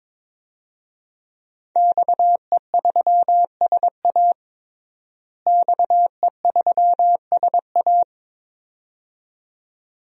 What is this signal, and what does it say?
Morse code, heard as a keyed tone, about 22 wpm, 700 Hz. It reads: XE3SA XE3SA
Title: DXCC-100 Call Signs - 5 Characters - Repeated 2x - 22wpm